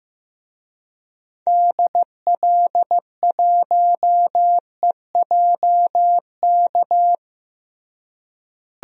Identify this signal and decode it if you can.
Morse code, heard as a keyed tone, about 15 wpm, 700 Hz. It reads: DL1EJK
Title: DXCC-100 Call Signs - 6 Characters - Repeated 3x - 15wpm